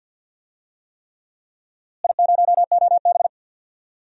Morse code, heard as a keyed tone, about 49 wpm, 700 Hz. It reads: I0OB
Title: DXCC-100 Call Signs - 4 Characters - Repeated 3x - 50wpm